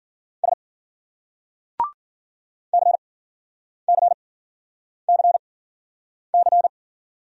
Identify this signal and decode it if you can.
Morse code, heard as a keyed tone, about 54 wpm, spaced as for 9 wpm, 700 Hz. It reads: S CCCC